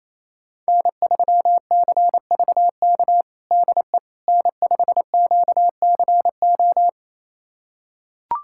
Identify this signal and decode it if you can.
Morse code, heard as a keyed tone, about 28 wpm, 700 Hz. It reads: N3CVK DE N5QCO
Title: USA Callsign DE Callsign - 5 Characters - 28wpm